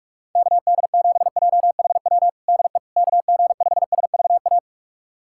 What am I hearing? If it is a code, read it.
Morse code, heard as a keyed tone, about 46 wpm, 700 Hz. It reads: KD7JHW DE KG5SVA
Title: USA Callsign DE Callsign - 6 Characters - 45wpm